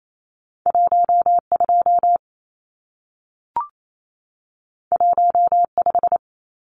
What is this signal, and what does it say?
Morse code, heard as a keyed tone, about 28 wpm, 700 Hz. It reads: 12 15
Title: Numbers - 2-Digit 28wpm